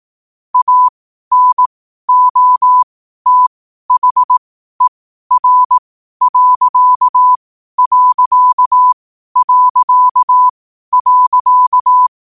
Morse code, heard as a keyed tone, about 18 wpm, spaced as for 12 wpm, 1000 Hz. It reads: ANOTHER....
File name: qotdfiles / 12wpm